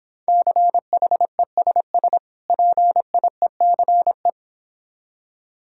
Morse code, heard as a keyed tone, about 26 wpm, 700 Hz. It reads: CHESS PIECE